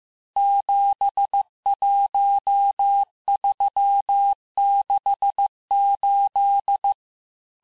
Morse code, heard as a keyed tone, about 15 wpm, 800 Hz. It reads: 71368
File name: cours-du-lundi files / cours-1585-texte-chiffre-vitesse-14mpm